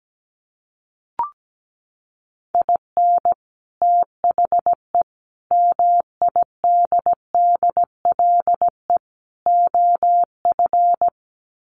Morse code, heard as a keyed tone, about 17 wpm, 700 Hz. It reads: IN THE MIDDLE OF
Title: Sets of 4 Words 17wpm